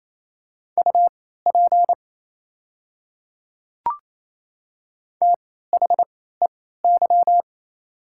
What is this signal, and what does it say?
Morse code, heard as a keyed tone, about 28 wpm, spaced as for 16 wpm, 700 Hz. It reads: UP THEY
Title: 3x Character Spacing - Top 100 Words - 28wpm